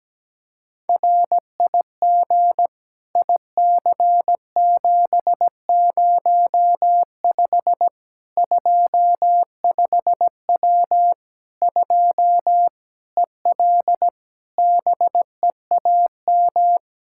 Morse code, heard as a keyed tone, about 17 wpm, 700 Hz. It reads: RIG IC705 25W 2 EL BEAM
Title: Practice QSO - Exchange 2 - Info - 17wpm